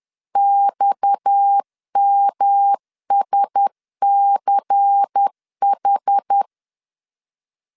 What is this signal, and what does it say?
Morse code, heard as a keyed tone, about 10 wpm, 800 Hz. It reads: XMSCH
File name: LEZIONI / 6 PKX0 / 6 PKX 10wpm 2426